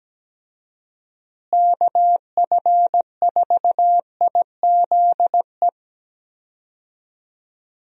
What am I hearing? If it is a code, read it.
Morse code, heard as a keyed tone, about 17 wpm, 700 Hz. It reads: KF4IZE
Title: USA Call Signs - 6 Characters - Repeated 2x - 17wpm